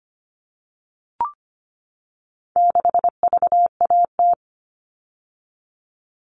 Morse code, heard as a keyed tone, about 25 wpm, 700 Hz. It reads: 6VAT
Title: All Cut Numbers - 4-Digit 25wpm